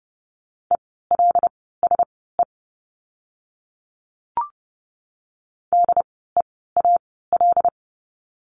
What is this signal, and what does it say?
Morse code, heard as a keyed tone, about 30 wpm, spaced as for 17 wpm, 700 Hz. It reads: ELSE DEAL
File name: Lesson 16f - Letter c - Words 30wpm